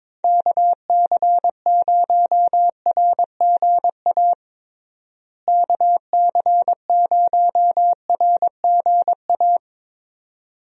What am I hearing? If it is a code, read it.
Morse code, heard as a keyed tone, about 21 wpm, 700 Hz. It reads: KC0RGA KC0RGA
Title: USA Call Signs - 6 Characters - Repeated 2x - 22wpm